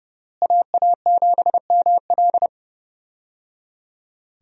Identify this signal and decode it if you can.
Morse code, heard as a keyed tone, about 30 wpm, 700 Hz. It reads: AA7ML
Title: USA Call Signs - 5 Characters - Repeated 3x - 30wpm